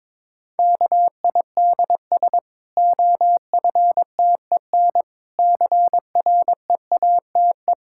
Morse code, heard as a keyed tone, about 22 wpm, 700 Hz. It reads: KIDS OFTEN CREATE
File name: Difficult Sentences with Hints - 22wpm